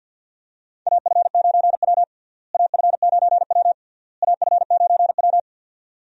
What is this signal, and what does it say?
Morse code, heard as a keyed tone, about 51 wpm, 700 Hz. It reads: AF9W AF9W AF9W
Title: USA Call Signs - 4 Characters - Repeated 3x - 50wpm